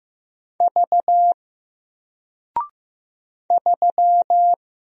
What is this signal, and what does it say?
Morse code, heard as a keyed tone, about 15 wpm, 700 Hz. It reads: V 3
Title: Single Character (Letter:Number:Punctuation:Prosign) 15wpm